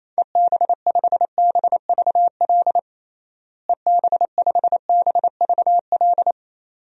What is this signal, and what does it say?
Morse code, heard as a keyed tone, about 28 wpm, 700 Hz. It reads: EB5BVL EB5BVL
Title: DXCC-100 Call Signs - 6 Characters - Repeated 2x - 28wpm